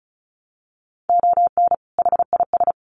Morse code, heard as a keyed tone, about 34 wpm, 700 Hz. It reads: ON HIS
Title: Sets of 2 Words 35wpm